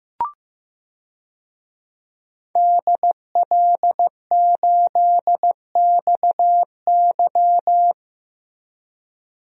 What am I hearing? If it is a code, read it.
Morse code, heard as a keyed tone, about 15 wpm, 700 Hz. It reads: DL8XY
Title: DXCC-100 Call Signs - 5 Characters - 15wpm